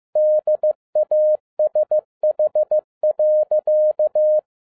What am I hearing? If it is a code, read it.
Morse code, heard as a keyed tone, about 15 wpm, 610 Hz. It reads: DASH.